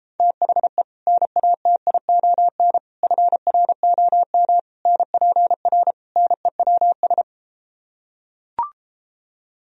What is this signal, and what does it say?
Morse code, heard as a keyed tone, about 33 wpm, 700 Hz. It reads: THE NATION FROM NPR NEWS